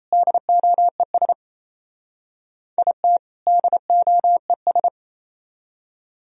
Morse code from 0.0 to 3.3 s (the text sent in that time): DOES IT